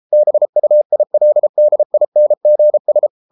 Morse code, heard as a keyed tone, about 33 wpm, 600 Hz. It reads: BUILDINGS